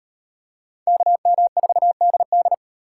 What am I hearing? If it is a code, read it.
Morse code, heard as a keyed tone, about 38 wpm, 700 Hz. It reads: KM4DD